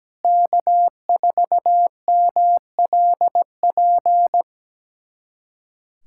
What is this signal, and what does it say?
Morse code, heard as a keyed tone, about 17 wpm, 700 Hz. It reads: K4MLP
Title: USA Callsign DE Callsign - 5 Characters - 17wpm